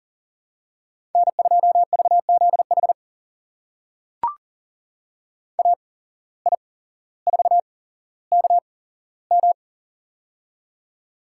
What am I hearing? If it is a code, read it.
Morse code, heard as a keyed tone, about 39 wpm, spaced as for 17 wpm, 700 Hz. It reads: N2VZH AI4KM
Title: USA Call Signs - 5 Characters - 8x Character Spacing - 40wpm